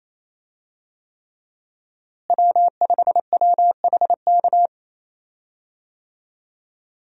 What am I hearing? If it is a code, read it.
Morse code, heard as a keyed tone, about 27 wpm, 700 Hz. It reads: W5WHK
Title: USA Call Signs - 5 Characters - Repeated 3x - 28wpm